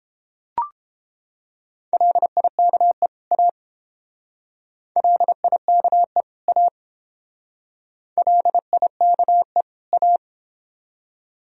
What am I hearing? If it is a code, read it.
Morse code, heard as a keyed tone, about 29 wpm, 700 Hz. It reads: LIKE A LIKE A LIKE A